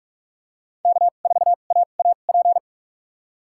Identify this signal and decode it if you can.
Morse code, heard as a keyed tone, about 46 wpm, spaced as for 33 wpm, 700 Hz. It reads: K4AAP
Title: USA Call Signs - 5 Characters - 2x Character Spacing - 45wpm